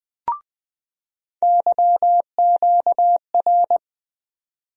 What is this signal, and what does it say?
Morse code, heard as a keyed tone, about 20 wpm, 700 Hz. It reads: YQR
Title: USA Call Sign Postfixes - 3 Letters 20wpm